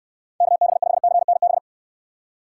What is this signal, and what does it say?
Morse code, heard as a keyed tone, about 69 wpm, 700 Hz. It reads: KB5CNB